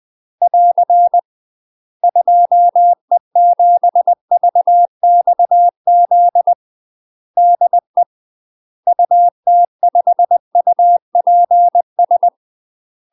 Morse code, heard as a keyed tone, about 20 wpm, 700 Hz. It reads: <AR> 2E7VXZ DE UT5UPS
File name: UK MASTER QSOs TRAINING / 20WPM / UK MASTER QSOs 0002